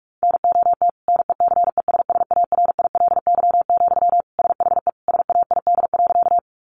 Morse code, heard as a keyed tone, about 44 wpm, 700 Hz. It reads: NOT NECESSARILY, SHE SAID.
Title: Sentences from Top 1400 Words 45wpm